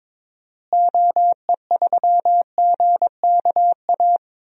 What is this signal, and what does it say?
Morse code, heard as a keyed tone, about 22 wpm, 700 Hz. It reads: OE3GKA
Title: DXCC-100 Call Signs - 6 Characters - Repeated 2x - 22wpm